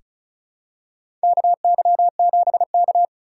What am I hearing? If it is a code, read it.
Morse code, heard as a keyed tone, about 36 wpm, 700 Hz. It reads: KY7K
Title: K1USN SST QSO Practice - 35wpm